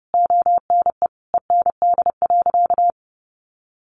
Morse code, heard as a keyed tone, about 30 wpm, 700 Hz. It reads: ONE END.
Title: Easy Phrases 30wpm